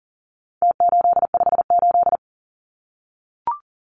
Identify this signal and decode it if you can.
Morse code, heard as a keyed tone, about 40 wpm, 700 Hz. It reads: T858
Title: Numbers - Contest Style - 4-Digit 40wpm